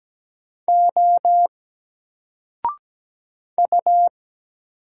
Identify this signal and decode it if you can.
Morse code, heard as a keyed tone, about 17 wpm, 700 Hz. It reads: O U
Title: Single Letter-Number Alt 17wpm